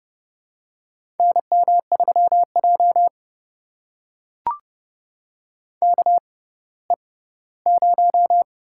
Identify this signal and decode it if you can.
Morse code, heard as a keyed tone, about 29 wpm, spaced as for 17 wpm, 700 Hz. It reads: NM3J KE0
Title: USA Call Signs - 6x Character Spacing - 30wpm